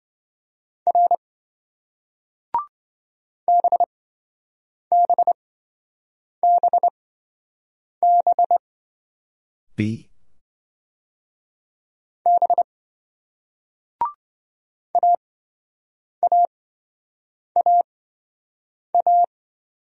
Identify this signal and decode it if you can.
Morse code, heard as a keyed tone, about 27 wpm, spaced as for 7 wpm, 700 Hz. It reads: R BBBB B AAAA